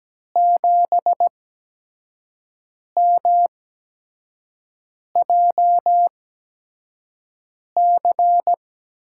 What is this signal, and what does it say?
Morse code, heard as a keyed tone, about 17 wpm, spaced as for 5 wpm, 700 Hz. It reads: 7MJC
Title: USA Call Signs - 8x Character Spacing - 17wpm